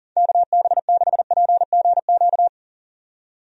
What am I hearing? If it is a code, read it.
Morse code, heard as a keyed tone, about 40 wpm, 700 Hz. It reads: KB6PGQ
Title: USA Call Signs - 6 Characters - 40wpm